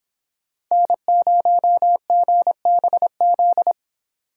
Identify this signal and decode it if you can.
Morse code, heard as a keyed tone, about 26 wpm, 700 Hz. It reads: N0GBZ
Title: Speed Racing - USA 1x3 Call Signs 22wpm